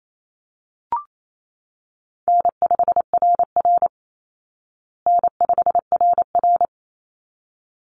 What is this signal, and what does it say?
Morse code, heard as a keyed tone, about 28 wpm, 700 Hz. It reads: N5RR N5RR
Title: Easy Call Signs Repeated 2x - 28wpm